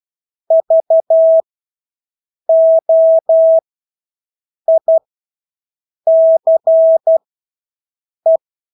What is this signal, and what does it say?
Morse code, heard as a keyed tone, about 12 wpm, spaced as for 6 wpm, 650 Hz. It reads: VOICE
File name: Engelske ord 12-6 wpm